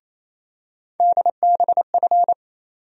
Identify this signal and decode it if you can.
Morse code, heard as a keyed tone, about 27 wpm, 700 Hz. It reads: DBF
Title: USA Call Sign Postfixes - 3 Letters 28wpm